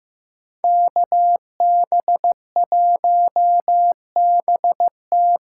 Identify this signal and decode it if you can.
Morse code, heard as a keyed tone, about 15 wpm, 700 Hz. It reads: KB1BT
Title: USA Call Signs - 6 Characters - 6x Character Spacing - 15wpm